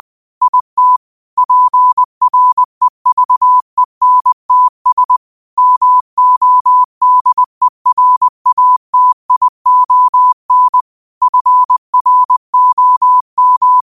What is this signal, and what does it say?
Morse code, heard as a keyed tone, about 20 wpm, 1000 Hz. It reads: IT PREVENTS MODERATION FROM